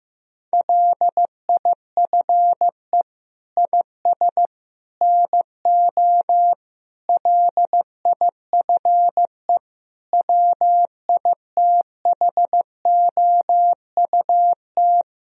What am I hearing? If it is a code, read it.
Morse code, heard as a keyed tone, about 15 wpm, 700 Hz. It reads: LIFE IS NO LIFE WITHOUT
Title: Sentences from Top 1000 Words - Review Part 3 - 15wpm